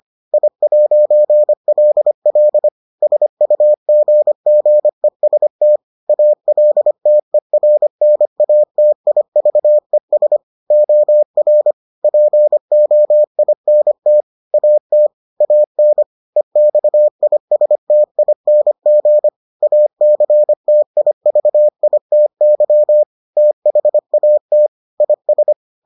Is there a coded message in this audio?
Yes: Morse code from 0.3 to 25.5 s, heard as a keyed tone, about 25 wpm, 600 Hz. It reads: I'LL SUGGEST ALTERNATIVES OR POINT AT AN EXISTING ACTIVITY THAT IS